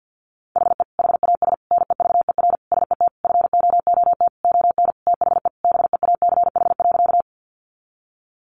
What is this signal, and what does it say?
Morse code, heard as a keyed tone, about 50 wpm, 700 Hz. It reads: HE HAS NEVER SET FOOT ON THE BEACH.
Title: Sentences from Top 1500 Words 50wpm